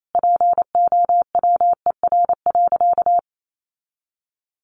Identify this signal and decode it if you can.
Morse code, heard as a keyed tone, about 28 wpm, 700 Hz. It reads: POWER.
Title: Sentences from Top 1500 Words 28wpm